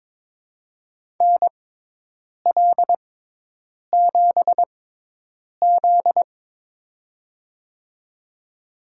Morse code, heard as a keyed tone, about 22 wpm, spaced as for 8 wpm, 700 Hz. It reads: NL7Z
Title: USA Call Signs - 6x Character Spacing - 22wpm